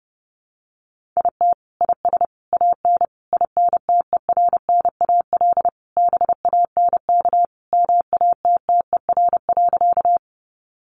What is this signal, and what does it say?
Morse code, heard as a keyed tone, about 30 wpm, 700 Hz. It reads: IT IS AN INTERNAL BANK MATTER.
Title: Sentences from Top 1400 Words 30wpm